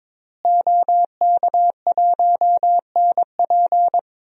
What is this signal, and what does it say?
Morse code, heard as a keyed tone, about 22 wpm, 700 Hz. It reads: OK1NP
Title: DXCC-100 Call Signs - 5 Characters - Repeated 3x - 22wpm